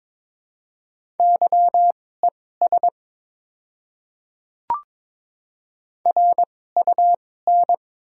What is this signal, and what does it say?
Morse code, heard as a keyed tone, about 22 wpm, spaced as for 16 wpm, 690 Hz. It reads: YES RUN
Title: 2x Character Spacing - Top 500 Words - 22wpm